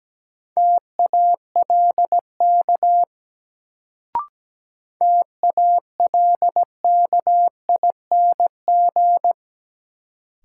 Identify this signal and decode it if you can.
Morse code, heard as a keyed tone, about 17 wpm, 700 Hz. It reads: TALK TALKING
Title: Sets of 2 Words - New Word Drill - 17wpm